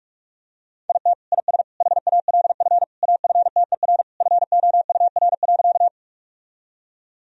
Morse code, heard as a keyed tone, about 45 wpm, 700 Hz. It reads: IT IS HALF AFTER FOUR.
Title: Sentences from Top 1000 Words - Review Part 1 - 45wpm